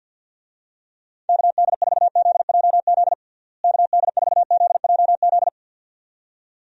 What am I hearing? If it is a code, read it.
Morse code, heard as a keyed tone, about 48 wpm, 700 Hz. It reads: KD4ZJZ KD4ZJZ
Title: USA Call Signs - 6 Characters - Repeated 2x - 50wpm